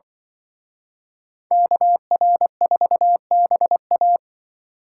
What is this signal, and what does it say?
Morse code, heard as a keyed tone, about 23 wpm, 700 Hz. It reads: KR4BA
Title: Speed Racing - USA 2x2 Call Signs 20wpm